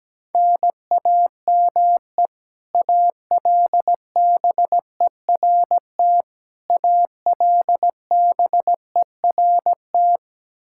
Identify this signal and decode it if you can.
Morse code, heard as a keyed tone, about 17 wpm, 700 Hz. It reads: NAME ALBERT ALBERT
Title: Practice QSO - Exchange 1 - Beginning - 17wpm